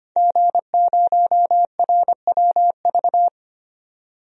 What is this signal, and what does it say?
Morse code, heard as a keyed tone, about 25 wpm, 700 Hz. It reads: G0RWV
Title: DXCC-100 Call Signs - 5 Characters - 25wpm